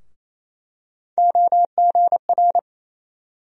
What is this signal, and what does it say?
Morse code, heard as a keyed tone, about 28 wpm, 700 Hz. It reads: OGR
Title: USA Call Sign Postfixes - 3 Letters 28wpm